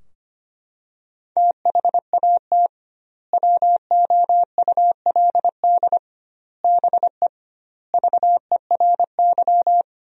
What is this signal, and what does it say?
Morse code, heard as a keyed tone, about 25 wpm, 700 Hz. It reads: THAT WOULD BE VERY
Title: 2x Word-Spacing - Sentences from Top 200 Words - 25wpm